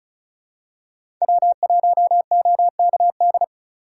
Morse code, heard as a keyed tone, about 34 wpm, 700 Hz. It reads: W1OKD
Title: USA Call Signs - 5 Characters - 4x Character Spacing - 35wpm